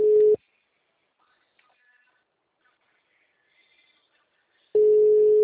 Ringback tone (Europe, 425 Hz).
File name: out- -2035- - - .6335